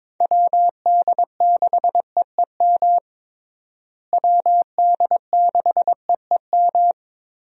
Morse code, heard as a keyed tone, about 22 wpm, 700 Hz. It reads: WD6EEM WD6EEM
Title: USA Call Signs - 6 Characters - Repeated 2x - 22wpm